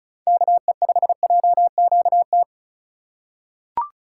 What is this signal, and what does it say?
Morse code, heard as a keyed tone, about 36 wpm, 700 Hz. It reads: KE5JQT
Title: USA Call Signs - 4x Character Spacing - 35wpm